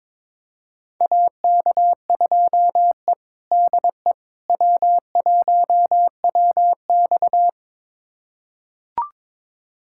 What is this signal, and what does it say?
Morse code, heard as a keyed tone, about 22 wpm, 700 Hz. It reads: AK2E DE W1WX
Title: USA Callsign DE Callsign - 4 Characters - 22wpm